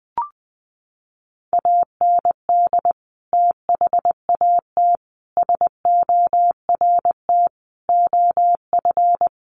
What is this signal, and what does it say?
Morse code, heard as a keyed tone, about 20 wpm, 700 Hz. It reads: AND THAT SORT OF
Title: Sets of 5 Words Encore2 20wpm